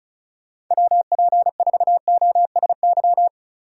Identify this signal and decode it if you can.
Morse code, heard as a keyed tone, about 44 wpm, 700 Hz. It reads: WP4OSY